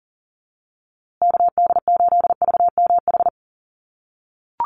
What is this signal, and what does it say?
Morse code, heard as a keyed tone, about 40 wpm, 700 Hz. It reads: KD8VMH
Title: USA Call Signs Repeated 3x 40wpm